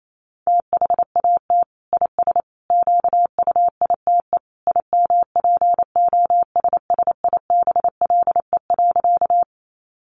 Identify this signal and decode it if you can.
Morse code, heard as a keyed tone, about 28 wpm, 700 Hz. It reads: THAT IS QUITE IMPOSSIBLE.
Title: Sentences from Top 1100 Words 28wpm